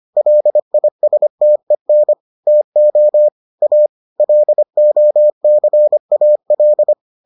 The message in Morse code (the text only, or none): LISTEN TO A LOCAL